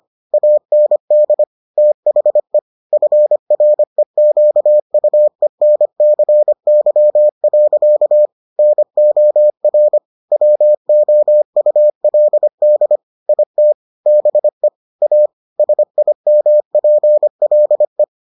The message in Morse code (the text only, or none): AND THE FREQUENCY. NOR WOULD IT BE A SIMPLE